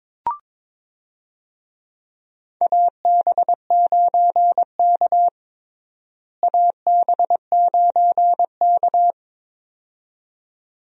Morse code, heard as a keyed tone, about 22 wpm, 700 Hz. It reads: AB9K AB9K
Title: USA Call Signs - 4 Characters - Repeated 2x - 22wpm